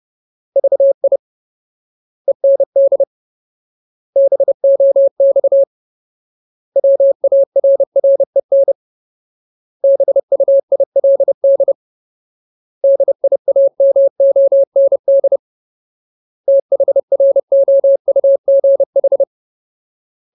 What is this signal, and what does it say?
Morse code, heard as a keyed tone, about 30 wpm, 550 Hz. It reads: VI END BOX WARREN BUILD DIAMOND THROUGH
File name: Ultimate Word List / 30WPM Ultimate